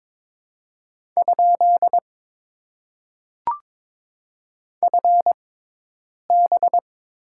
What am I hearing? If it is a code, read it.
Morse code, heard as a keyed tone, about 22 wpm, spaced as for 8 wpm, 700 Hz. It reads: ? FB
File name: Top 90 QSO Elements - 6x Character Spacing - 22wpm